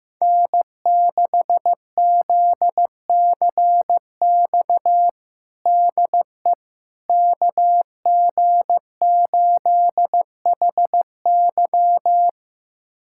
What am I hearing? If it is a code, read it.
Morse code, heard as a keyed tone, about 15 wpm, 700 Hz. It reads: N6ZCX DE KG8HY
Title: USA Callsign DE Callsign 15wpm